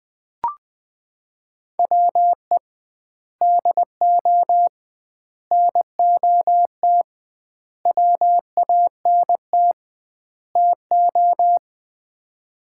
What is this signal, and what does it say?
Morse code, heard as a keyed tone, about 20 wpm, 700 Hz. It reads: WE DO NOT WANT TO
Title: 2x Word-Spacing - Sets of 5 Words - 20wpm